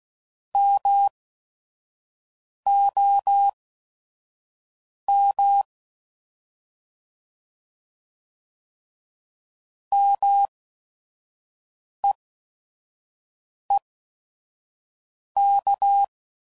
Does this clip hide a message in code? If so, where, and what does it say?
Yes: Morse code from 0.6 to 16.0 s, heard as a keyed tone, about 16 wpm, spaced as for 5 wpm, 800 Hz. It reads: MOM MEEK